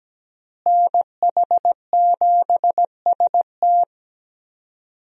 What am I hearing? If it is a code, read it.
Morse code, heard as a keyed tone, about 17 wpm, 700 Hz. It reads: NH7ST
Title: USA Call Signs - 5 Characters - Repeated 3x - 17wpm